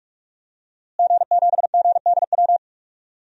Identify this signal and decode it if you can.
Morse code, heard as a keyed tone, about 47 wpm, 700 Hz. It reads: G7GDW
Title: DXCC-100 Call Signs - 5 Characters - Repeated 2x - 45wpm